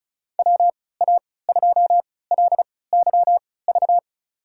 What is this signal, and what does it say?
Morse code, heard as a keyed tone, about 36 wpm, spaced as for 20 wpm, 700 Hz. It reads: WA2LYV
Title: USA Call Signs - 6 Characters - 3x Character Spacing - 35wpm